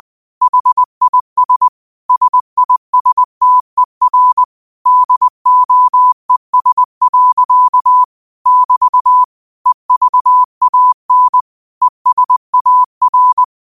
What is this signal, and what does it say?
Morse code, heard as a keyed tone, about 20 wpm, 1000 Hz. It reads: HIS SISTER DOES. <BT> EVAN ESAR
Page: Quotes for Tue, 12 Aug 2025 in Morse Code at 20 words per minute.